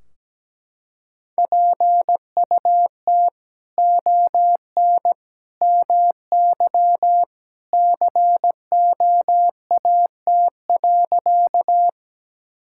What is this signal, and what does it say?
Morse code, heard as a keyed tone, about 17 wpm, 700 Hz. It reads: PUT ON MY COAT.
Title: Sentences from Top 1600 Words 17wpm